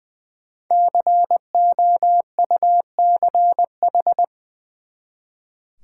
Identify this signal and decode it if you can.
Morse code, heard as a keyed tone, about 21 wpm, 700 Hz. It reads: COUCH
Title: Easy Words with Hints - 20wpm